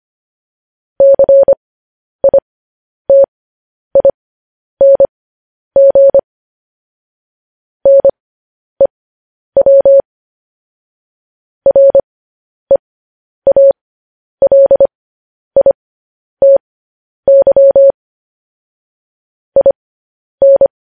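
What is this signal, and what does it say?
Morse code, heard as a keyed tone, about 25 wpm, spaced as for 10 wpm, 550 Hz. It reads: CITING NEW REALITY IN
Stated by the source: Morse code practice sending news headlines at 10 wpm.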